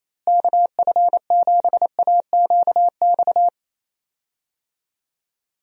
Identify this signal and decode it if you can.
Morse code, heard as a keyed tone, about 28 wpm, 700 Hz. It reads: KF7AQX
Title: USA Call Signs - 6 Characters - Repeated 3x - 28wpm